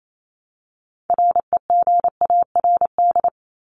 Morse code, heard as a keyed tone, about 28 wpm, 700 Hz. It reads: REGARD
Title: ING Suffix - Top 100 Words with Base Words 28wpm